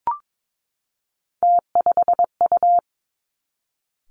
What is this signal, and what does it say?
Morse code, heard as a keyed tone, about 22 wpm, 700 Hz. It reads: T5U
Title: Three Challenging Letter-Number Combinations 22wpm